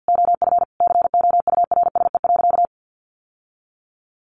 Morse code, heard as a keyed tone, about 50 wpm, 700 Hz. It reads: OF COURSE.
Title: Sentences from Top 900 Words 50wpm